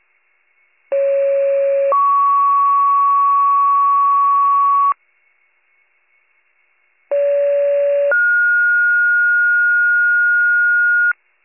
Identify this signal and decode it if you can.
Two-tone page, heard as a one-second tone, then a three-second tone, about 550 then 1120 Hz, about 570 then 1430 Hz. It is heard from 0.9 to 11.1 s.